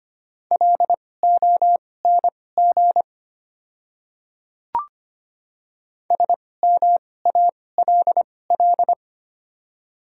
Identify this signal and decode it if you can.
Morse code, heard as a keyed tone, about 25 wpm, spaced as for 18 wpm, 700 Hz. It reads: LONG SMALL
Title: 2x Character Spacing - Top 300 Words - 25wpm